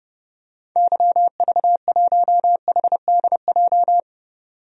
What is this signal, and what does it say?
Morse code, heard as a keyed tone, about 30 wpm, 700 Hz. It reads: YV1HDJ
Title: DXCC-100 Call Signs - 6 Characters - 30wpm